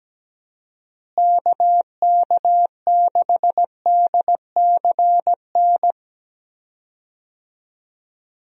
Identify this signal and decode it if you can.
Morse code, heard as a keyed tone, about 17 wpm, 700 Hz. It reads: KK6DCN
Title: USA Call Signs - 6 Characters - Repeated 3x - 17wpm